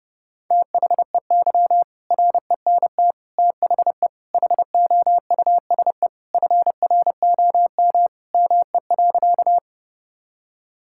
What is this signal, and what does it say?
Morse code, heard as a keyed tone, about 30 wpm, 700 Hz. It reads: THEY RENT THE HOUSE FROM ME.
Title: Sentences from Top 1000 Words - Review Part 5 - 30wpm